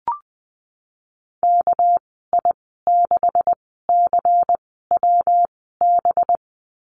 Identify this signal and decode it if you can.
Morse code, heard as a keyed tone, about 19 wpm, 700 Hz. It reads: K I 6 C W B
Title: 200 Lesson 32g - Letter k - Call Signs 20wpm